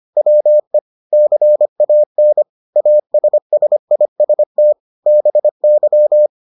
Morse code, heard as a keyed tone, about 25 wpm, 600 Hz. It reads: WE CAN ASSIST BY